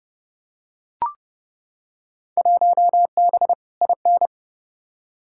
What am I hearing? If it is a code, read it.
Morse code, heard as a keyed tone, about 31 wpm, 700 Hz. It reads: 1B IN
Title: Field Day Exchanges - 30wpm